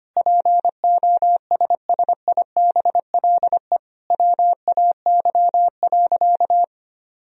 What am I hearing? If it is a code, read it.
Morse code, heard as a keyed tone, about 25 wpm, 700 Hz. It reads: POSSIBLE WAY.
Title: Sentences from Top 500 Words and Abbreviations 25wpm